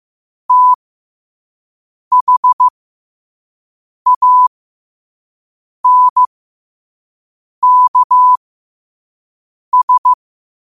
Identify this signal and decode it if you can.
Morse code, heard as a keyed tone, about 15 wpm, spaced as for 5 wpm, 1000 Hz. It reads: THANKS
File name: qotdfiles / 5wpm